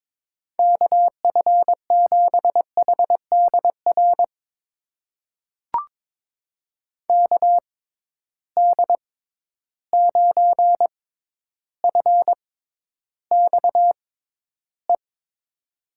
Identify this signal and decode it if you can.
Morse code, heard as a keyed tone, about 22 wpm, spaced as for 11 wpm, 700 Hz. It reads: KF7HDR KD9FXE
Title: USA Call Signs - 6 Characters - 6x Character Spacing - 22wpm